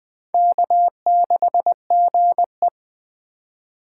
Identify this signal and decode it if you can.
Morse code, heard as a keyed tone, about 20 wpm, 700 Hz. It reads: K6GE
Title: USA Call Signs - 2x Character Spacing - 20wpm